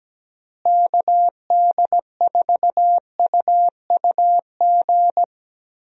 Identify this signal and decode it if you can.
Morse code, heard as a keyed tone, about 17 wpm, 700 Hz. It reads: KD4UUG
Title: USA Call Signs - 6 Characters - Repeated 2x - 17wpm